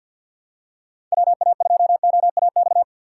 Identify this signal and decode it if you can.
Morse code, heard as a keyed tone, about 52 wpm, 700 Hz. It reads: WA2OAX
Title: USA Call Signs - 6 Characters - 3x Character Spacing - 50wpm